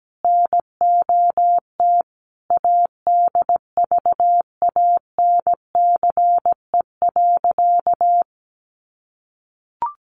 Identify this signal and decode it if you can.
Morse code, heard as a keyed tone, about 17 wpm, 700 Hz. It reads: NOT ADVANCE.
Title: Sentences from Top 1400 Words 17wpm